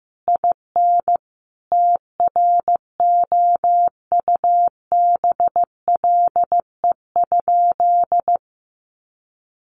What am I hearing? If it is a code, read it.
Morse code, heard as a keyed tone, about 15 wpm, 700 Hz. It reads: IN TROUBLE?
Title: Sentences from Top 1200 Words 15wpm